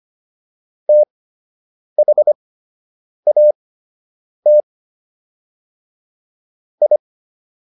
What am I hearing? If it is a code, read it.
Morse code, heard as a keyed tone, about 25 wpm, spaced as for 8 wpm, 600 Hz. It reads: THAT I